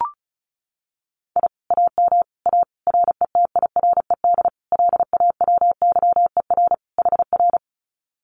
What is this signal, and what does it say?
Morse code, heard as a keyed tone, about 34 wpm, 700 Hz. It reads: I AM A RETIRED LAWYER HR
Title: Common QSO Phrases Encore 35wpm